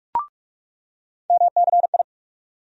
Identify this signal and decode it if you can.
Morse code, heard as a keyed tone, about 47 wpm, 700 Hz. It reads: MCI
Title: USA Call Sign Postfixes - 3 Letters 45wpm